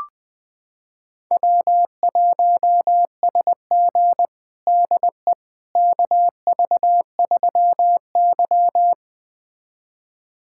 Morse code, heard as a keyed tone, about 21 wpm, 700 Hz. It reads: W1SG DE KV3Y
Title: USA Callsign DE Callsign - 4 Characters - 20wpm